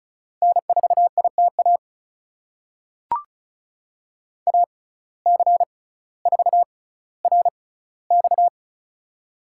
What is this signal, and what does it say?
Morse code, heard as a keyed tone, about 36 wpm, spaced as for 18 wpm, 700 Hz. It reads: N4ITA AC4RX
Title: USA Call Signs - 5 Characters - 6x Character Spacing - 35wpm